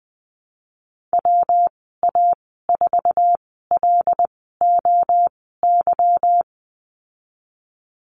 Morse code, heard as a keyed tone, about 20 wpm, spaced as for 15 wpm, 700 Hz. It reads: WA4LOY
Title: 147 Lesson 24g - Letter y - Call Signs 20wpm